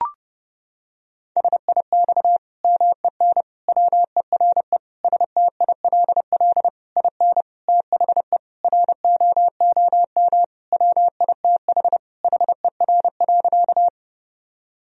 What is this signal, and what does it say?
Morse code, heard as a keyed tone, about 32 wpm, 700 Hz. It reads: SIX MEN WERE STILL IN THE ROOM WITH HER.
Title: Sentences from Top 1000 Words - Review Part 2 - 30wpm